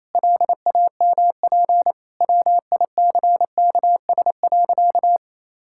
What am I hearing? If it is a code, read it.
Morse code, heard as a keyed tone, about 28 wpm, 700 Hz. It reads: LAMP WICKS.
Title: Pangram Challenge v2 - 28wpm